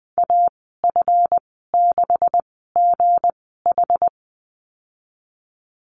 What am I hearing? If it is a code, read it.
Morse code, heard as a keyed tone, about 20 wpm, spaced as for 15 wpm, 700 Hz. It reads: AF6GH
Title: 167 Lesson 27g - Number 7 - Call Signs 20wpm